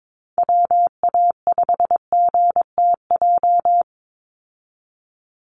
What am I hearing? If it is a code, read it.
Morse code, heard as a keyed tone, about 22 wpm, 700 Hz. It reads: WA5GTJ
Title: USA Call Signs 22wpm